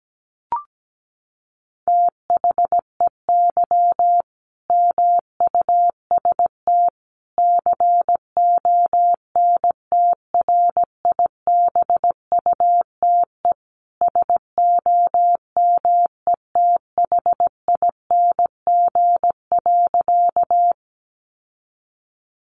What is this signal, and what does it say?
Morse code, heard as a keyed tone, about 17 wpm, 700 Hz. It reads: THEY MUST CONTRIBUTE SOMETHING.
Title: Sentences from Top 1400 Words 17wpm